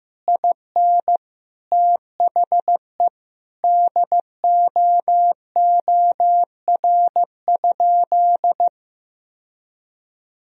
Morse code, heard as a keyed tone, about 15 wpm, 700 Hz. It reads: IN THE DOOR?
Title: Sentences from Top 1000 Words - Review Part 4 - 15wpm